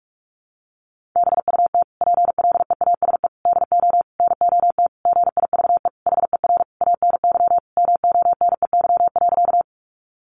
Morse code, heard as a keyed tone, about 44 wpm, 700 Hz. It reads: BUT PLEASE DO NOT GIVE HER ANY MONEY.
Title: Sentences from Top 1500 Words 45wpm